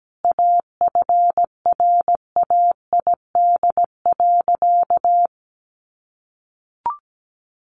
Morse code, heard as a keyed tone, about 17 wpm, 700 Hz. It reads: AFRAID.
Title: Sentences from Top 1500 Words 17wpm